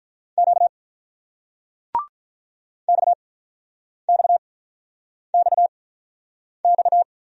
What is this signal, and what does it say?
Morse code, heard as a keyed tone, about 49 wpm, spaced as for 9 wpm, 700 Hz. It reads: Q XXXX